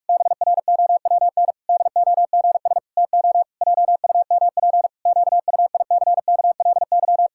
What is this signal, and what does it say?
Morse code, heard as a keyed tone, about 46 wpm, 700 Hz. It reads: BROWN DOGS TO JUMP QUICKLY
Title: Pangram Challenge v2 - 45wpm